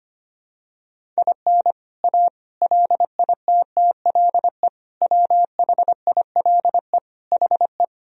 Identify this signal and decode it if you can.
Morse code, heard as a keyed tone, about 25 wpm, 700 Hz. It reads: IN A LITTLE WHILE HE
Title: Sentences from Top 1000 Words - Review Part 1 - 25wpm